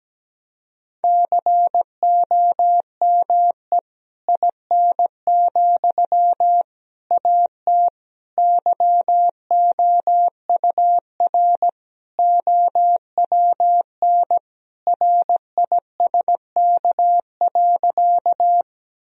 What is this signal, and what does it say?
Morse code, heard as a keyed tone, about 17 wpm, 700 Hz. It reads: COME IN, AT YOUR OWN RISK.
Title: Sentences from Top 1000 Words - Review Part 4 - 17wpm